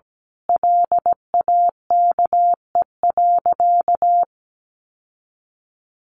Morse code, heard as a keyed tone, about 17 wpm, 700 Hz. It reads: LAKE.
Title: Sentences from Top 1100 Words 17wpm